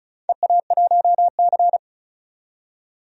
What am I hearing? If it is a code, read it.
Morse code, heard as a keyed tone, about 36 wpm, 700 Hz. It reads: EA1C
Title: DXCC-100 Call Signs - 4 Characters - 35wpm